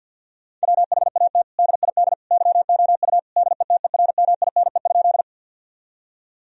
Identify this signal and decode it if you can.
Morse code, heard as a keyed tone, about 50 wpm, 700 Hz. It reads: WHAT DID YOU DETERMINE?
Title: Sentences from Top 1000 Words - Review Part 4 - 50wpm